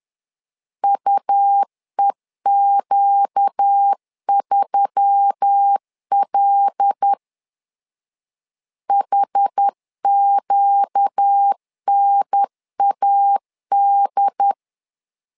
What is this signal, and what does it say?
Morse code, heard as a keyed tone, about 10 wpm, 800 Hz. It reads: UEQ3L HQNAD
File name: LEZIONI / 5 LBVC / 5 L 10wpm 2259